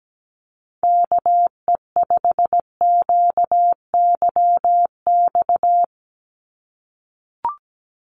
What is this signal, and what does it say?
Morse code, heard as a keyed tone, about 17 wpm, 700 Hz. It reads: KE5QYX
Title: USA Call Signs 17wpm